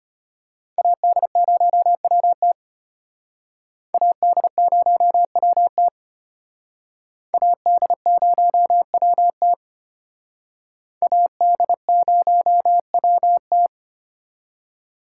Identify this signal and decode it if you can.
Morse code, heard as a keyed tone, about 34 wpm, 700 Hz. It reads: AD0WT AD0WT AD0WT AD0WT